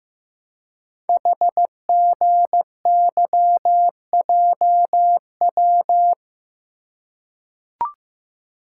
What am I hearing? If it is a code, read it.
Morse code, heard as a keyed tone, about 15 wpm, 700 Hz. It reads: HGYJW
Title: Code Groups - 5 Letters - 15wpm